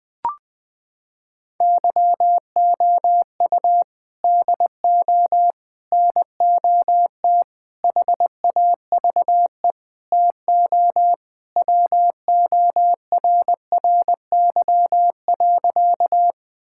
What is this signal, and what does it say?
Morse code, heard as a keyed tone, about 19 wpm, 700 Hz. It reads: YOU DO NOT HAVE TO WORRY.
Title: Sentences with 5-Word Phrases - 20wpm